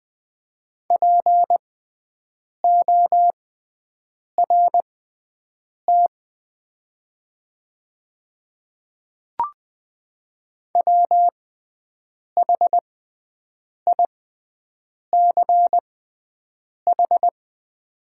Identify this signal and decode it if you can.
Morse code, heard as a keyed tone, about 20 wpm, spaced as for 7 wpm, 700 Hz. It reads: PORT WHICH
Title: 6x Character Spacing - Top 200 Words - 20wpm